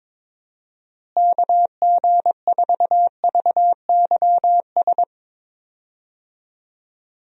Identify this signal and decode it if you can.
Morse code, heard as a keyed tone, about 22 wpm, 700 Hz. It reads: KG4VYS
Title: USA Call Signs - 6 Characters - Repeated 2x - 22wpm